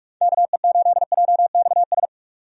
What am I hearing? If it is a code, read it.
Morse code, heard as a keyed tone, about 43 wpm, 700 Hz. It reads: KE8JXS
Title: USA Call Signs - 6 Characters - 45wpm